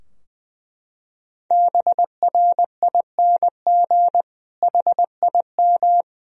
Morse code, heard as a keyed tone, about 20 wpm, 700 Hz. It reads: BRING HIM
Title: Sentences from Top 1000 Words - Review Part 3 - 20wpm